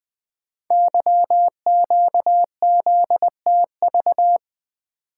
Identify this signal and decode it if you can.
Morse code, heard as a keyed tone, about 20 wpm, 700 Hz. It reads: YQZTV
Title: Code Groups - 5 Letter:Numbers - 20wpm